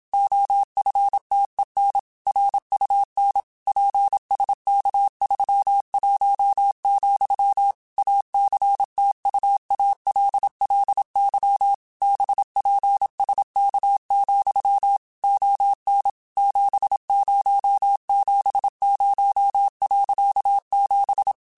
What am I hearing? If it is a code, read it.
Morse code, heard as a keyed tone, about 27 wpm, 800 Hz. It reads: OFTEN RUN PSK31, ACTUALLY BPSK, ON 7070.7